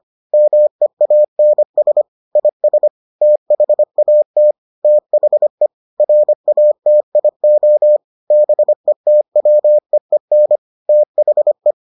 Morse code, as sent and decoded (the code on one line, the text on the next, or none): -- . .- -. ... / .. ... / - .... .- - / - .... / .-. .- - .. --- / -... - .-- . -. / - .... .
MEANS IS THAT THE RATIO BETWEEN THE